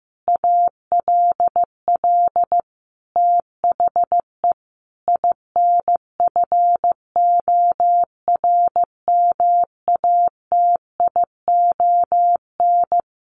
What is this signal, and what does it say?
Morse code, heard as a keyed tone, about 15 wpm, 700 Hz. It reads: ALL THE INFORMATION
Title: Sentences from Top 600 Words 15wpm